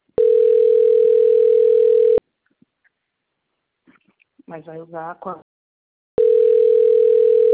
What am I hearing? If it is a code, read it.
Ringback tone (North America), 440 + 480 Hz, with speech.
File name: rg-1600-unknown- - - .46335